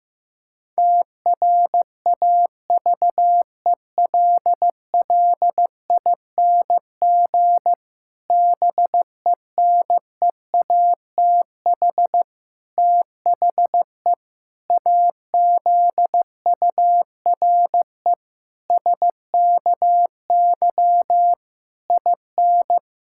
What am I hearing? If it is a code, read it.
Morse code, heard as a keyed tone, about 15 wpm, 700 Hz. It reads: TRAVELLING BENEATH THE AZURE SKY IN
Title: Pangram Challenge v2 - 15wpm